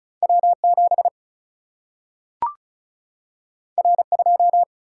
Morse code, heard as a keyed tone, about 36 wpm, 700 Hz. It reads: W 7 R 2
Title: Single-Letter Single-Number 35wpm